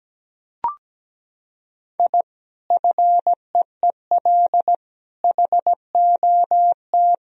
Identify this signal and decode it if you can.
Morse code, heard as a keyed tone, about 17 wpm, 700 Hz. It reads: I FEEL HOT
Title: Sentences from Top 1000 Words - Review Part 3 - 17wpm